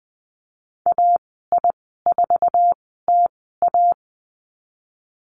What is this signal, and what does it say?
Morse code, heard as a keyed tone, about 20 wpm, spaced as for 14 wpm, 700 Hz. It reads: AI4TA
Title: 048 Lesson 9g - Number 4 - Call Signs 20wpm